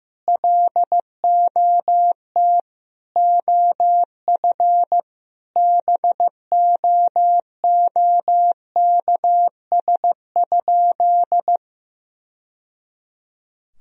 Morse code, heard as a keyed tone, about 15 wpm, 700 Hz. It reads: LOT OF BOOKS?
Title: Sentences from Top 1000 Words - Review Part 4 - 15wpm